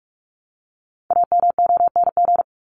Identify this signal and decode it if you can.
Morse code, heard as a keyed tone, about 44 wpm, 700 Hz. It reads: AMONG
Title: Top 500 Words 45wpm